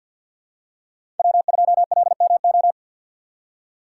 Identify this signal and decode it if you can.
Morse code, heard as a keyed tone, about 48 wpm, 700 Hz. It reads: W2LMO
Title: USA Call Signs - 6x Character Spacing - 50wpm